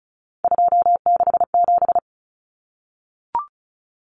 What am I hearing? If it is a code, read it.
Morse code, heard as a keyed tone, about 34 wpm, 700 Hz. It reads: 267
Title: Common Cut Numbers - 3-Digit 35wpm